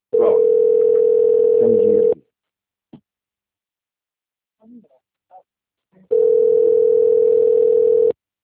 Ringback tone (North America), 440 + 480 Hz.